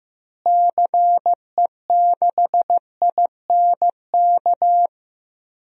Morse code, heard as a keyed tone, about 15 wpm, 700 Hz. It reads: CE6INK
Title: DXCC-100 Call Signs - 6 Characters - Repeated 2x - 15wpm